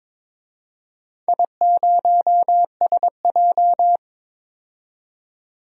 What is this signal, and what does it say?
Morse code, heard as a keyed tone, about 22 wpm, 700 Hz. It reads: I0SJ
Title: DXCC-100 Call Signs - 4 Characters - 22wpm